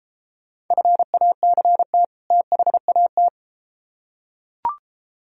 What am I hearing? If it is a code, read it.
Morse code, heard as a keyed tone, about 32 wpm, 700 Hz. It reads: FACT THAT